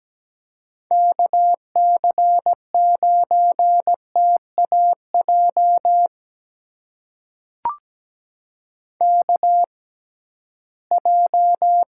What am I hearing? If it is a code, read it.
Morse code, heard as a keyed tone, about 17 wpm, spaced as for 13 wpm, 700 Hz. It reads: KC9TAJ KJ
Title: USA Call Signs - 6 Characters - 6x Character Spacing - 17wpm